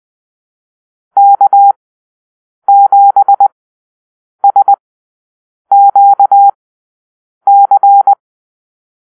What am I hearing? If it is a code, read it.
Morse code, heard as a keyed tone, about 20 wpm, spaced as for 7 wpm, 800 Hz. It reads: K7SQC
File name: arrl / Call-Sign Drill-10